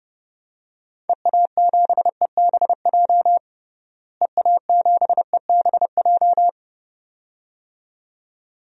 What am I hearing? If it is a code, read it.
Morse code, heard as a keyed tone, about 30 wpm, 700 Hz. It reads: EA7EBJ EA7EBJ
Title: DXCC-100 Call Signs - 6 Characters - Repeated 2x - 30wpm